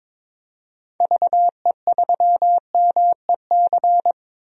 Morse code, heard as a keyed tone, about 22 wpm, 700 Hz. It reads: VE3MEC
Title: DXCC-100 Call Signs - 6 Characters - Repeated 3x - 22wpm